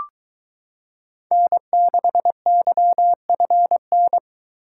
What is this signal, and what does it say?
Morse code, heard as a keyed tone, about 22 wpm, 700 Hz. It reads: N6YFN